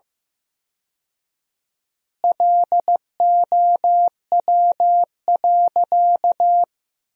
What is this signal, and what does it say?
Morse code, heard as a keyed tone, about 15 wpm, 700 Hz. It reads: LOW.
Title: 4x Word-Spacing - Sentences from Top 500 Words - 15wpm